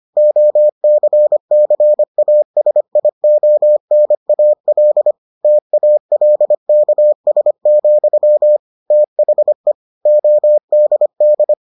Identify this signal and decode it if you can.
Morse code, heard as a keyed tone, about 25 wpm, 600 Hz. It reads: OCCASIONAL TALKS, THE ODD